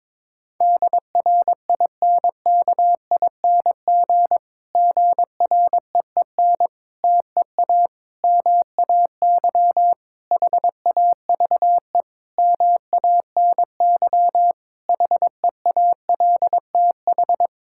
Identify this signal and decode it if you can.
Morse code, heard as a keyed tone, about 22 wpm, 700 Hz. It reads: DRINKING GREEN TEA MAY HAVE MANY HEALTH
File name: Difficult Sentences with Hints - 22wpm